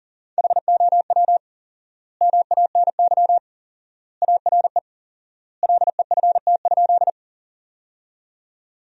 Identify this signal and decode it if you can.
Morse code, heard as a keyed tone, about 40 wpm, 700 Hz. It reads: HOW MANY ARE LEFT?
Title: 4x Word-Spacing - Sentences from Top 300 Words - 40wpm